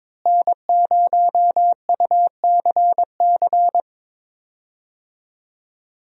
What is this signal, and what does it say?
Morse code, heard as a keyed tone, about 22 wpm, 700 Hz. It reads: N0UCC
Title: USA Call Signs - 5 Characters - Repeated 2x - 22wpm